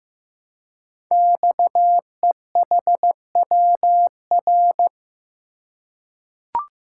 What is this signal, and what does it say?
Morse code, heard as a keyed tone, about 15 wpm, 700 Hz. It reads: XEHWR
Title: Code Groups - 5 Letters - 15wpm